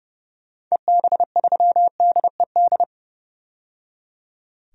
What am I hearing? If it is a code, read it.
Morse code, heard as a keyed tone, about 32 wpm, 700 Hz. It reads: EB3DED
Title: DXCC-100 Call Signs - 6 Characters - 30wpm